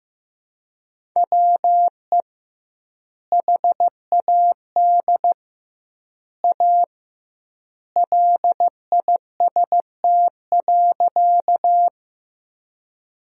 Morse code, heard as a keyed tone, about 15 wpm, 700 Hz. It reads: WE HAD A LIST.
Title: 2x Word-Spacing - Sentences from Top 400 Words - 15wpm